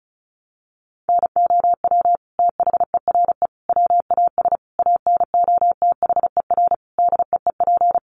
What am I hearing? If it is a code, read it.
Morse code, heard as a keyed tone, about 34 wpm, 700 Hz. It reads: NOW THERE WAS ANOTHER DEEP
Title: Sentences from Top 1300 Words 35wpm